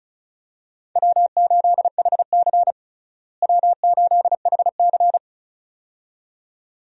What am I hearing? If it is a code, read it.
Morse code, heard as a keyed tone, about 36 wpm, 700 Hz. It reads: W8HC W8HC
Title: USA Call Signs - 4 Characters - Repeated 2x - 35wpm